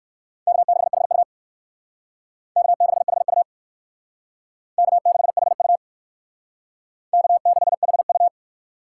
Morse code, heard as a keyed tone, about 64 wpm, 700 Hz. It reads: K6HU K6HU K6HU K6HU